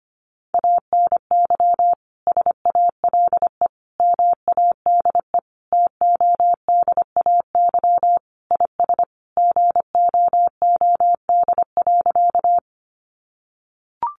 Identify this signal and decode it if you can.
Morse code, heard as a keyed tone, about 25 wpm, 700 Hz. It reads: ANY SALE MADE TODAY IS GOOD.
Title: Sentences from Top 1400 Words 25wpm